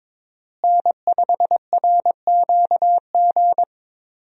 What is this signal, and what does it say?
Morse code, heard as a keyed tone, about 23 wpm, 700 Hz. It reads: N5RQG
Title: USA Call Signs - 5 Characters - Repeated 3x - 22wpm